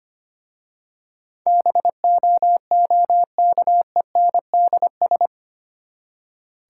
Morse code, heard as a keyed tone, about 25 wpm, 690 Hz. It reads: BOOKENDS
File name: practice / 25wpm / Easy Words with Hints - 25wpm